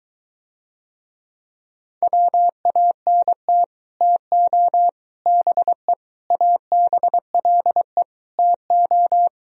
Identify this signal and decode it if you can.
Morse code, heard as a keyed tone, about 23 wpm, 700 Hz. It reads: WANT TO BE ABLE TO